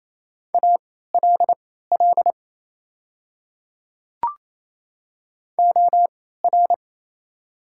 Morse code, heard as a keyed tone, about 27 wpm, spaced as for 16 wpm, 690 Hz. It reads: ALL OR
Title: 3x Character Spacing - Top 100 Words - 28wpm